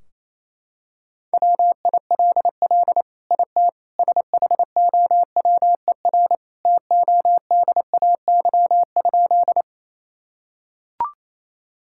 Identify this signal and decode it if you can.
Morse code, heard as a keyed tone, about 28 wpm, 700 Hz. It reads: WILL IT SHOWER TODAY?
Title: Sentences from Top 1600 Words 28wpm